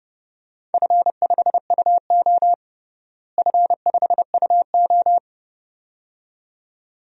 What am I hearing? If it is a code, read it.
Morse code, heard as a keyed tone, about 29 wpm, 700 Hz. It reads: F5UO F5UO
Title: DXCC-100 Call Signs - 4 Characters - Repeated 2x - 30wpm